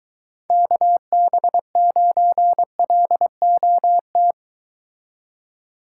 Morse code, heard as a keyed tone, about 23 wpm, 700 Hz. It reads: KB9LOT